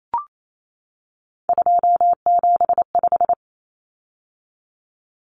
Morse code, heard as a keyed tone, about 28 wpm, 700 Hz. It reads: 275
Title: Common Cut Numbers - 3-Digit 28wpm